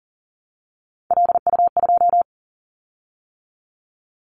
Morse code, heard as a keyed tone, about 40 wpm, 700 Hz. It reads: LU2
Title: Callsign Prefixes for Top 100 DXCC - 40wpm